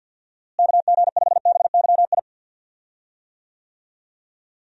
Morse code, heard as a keyed tone, about 51 wpm, 700 Hz. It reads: KG5BYI
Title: USA Call Signs - 6 Characters - Repeated 3x - 50wpm